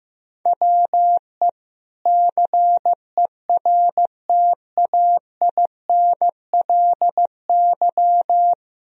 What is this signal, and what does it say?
Morse code, heard as a keyed tone, about 16 wpm, 700 Hz. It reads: WE CERTAINLY
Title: Sentences from Top 1000 Words - Review Part 4 - 15wpm